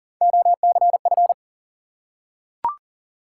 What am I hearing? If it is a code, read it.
Morse code, heard as a keyed tone, about 42 wpm, 700 Hz. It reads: OCF
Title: Top 180 QSO Elements 40wpm